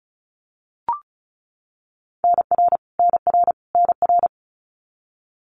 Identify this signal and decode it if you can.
Morse code, heard as a keyed tone, about 34 wpm, 700 Hz. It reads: NR NR NR
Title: Abbreviations Repeated 3x - 35wpm